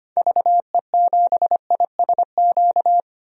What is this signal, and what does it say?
Morse code, heard as a keyed tone, about 25 wpm, 700 Hz. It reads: VE7ISQ
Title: DXCC-100 Call Signs - 6 Characters - 25wpm